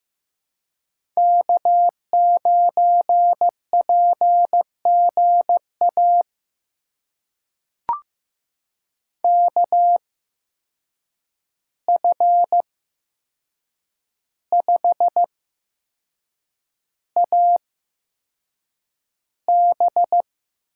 Morse code, heard as a keyed tone, about 15 wpm, spaced as for 6 wpm, 700 Hz. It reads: K9PGA KF5AB
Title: USA Call Signs - 5 Characters - 8x Character Spacing - 15wpm